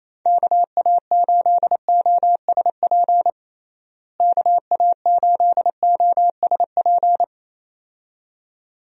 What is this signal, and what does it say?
Morse code, heard as a keyed tone, about 28 wpm, 700 Hz. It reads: KA8OSP KA8OSP
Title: USA Call Signs - 6 Characters - Repeated 2x - 28wpm